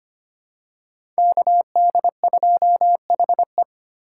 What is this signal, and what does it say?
Morse code, heard as a keyed tone, about 25 wpm, 700 Hz. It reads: KD2HE
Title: K1USN SST QSO Practice - 25wpm